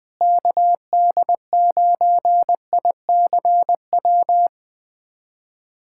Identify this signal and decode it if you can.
Morse code, heard as a keyed tone, about 20 wpm, 700 Hz. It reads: KD9ICW
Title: USA Callsign DE Callsign 20wpm